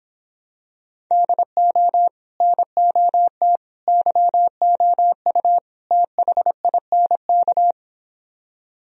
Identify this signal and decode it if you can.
Morse code, heard as a keyed tone, about 26 wpm, 700 Hz. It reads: DO NOT YOU THINK